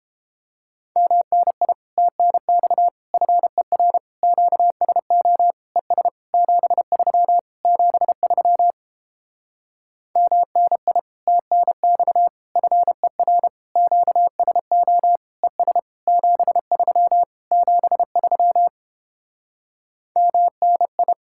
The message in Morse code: MNI TNX FER QSO ES 73 73 MNI TNX FER QSO ES 73 73 MNI

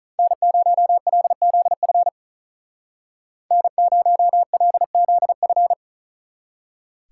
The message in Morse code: N0LZF N0LZF

